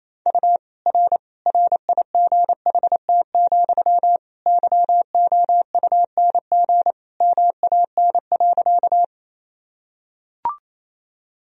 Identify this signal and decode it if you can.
Morse code, heard as a keyed tone, about 27 wpm, 700 Hz. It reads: U R RIGHT, YOUNG MAN.
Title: Sentences from Top 400 Words and Abbreviations 28wpm